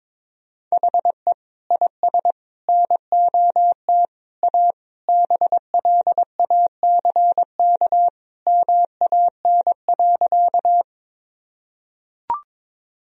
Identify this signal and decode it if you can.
Morse code, heard as a keyed tone, about 21 wpm, 700 Hz. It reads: HE IS NOT A BLACK MAN.
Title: Sentences from Top 1000 Words - Review Part 2 - 22wpm